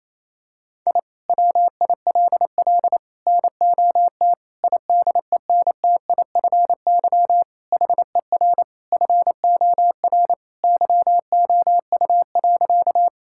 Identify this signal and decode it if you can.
Morse code, heard as a keyed tone, about 27 wpm, 700 Hz. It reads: I WILL NOT IDENTIFY HER FOR YOU.
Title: Sentences from Top 1000 Words - Review Part 6 - 28wpm